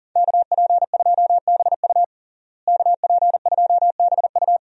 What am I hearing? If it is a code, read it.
Morse code, heard as a keyed tone, about 40 wpm, 700 Hz. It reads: KP2BU KP2BU
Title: USA Call Signs - 5 Characters - Repeated 3x - 40wpm